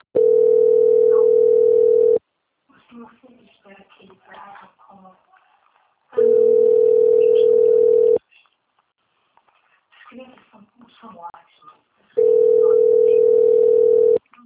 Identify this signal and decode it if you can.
Ringback tone (North America), 440 + 480 Hz, with speech.
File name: out- - - - - .75143